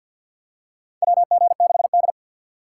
Morse code, heard as a keyed tone, about 45 wpm, 700 Hz. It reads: WG6D
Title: USA Call Signs - 4 Characters - 3x Character Spacing - 50wpm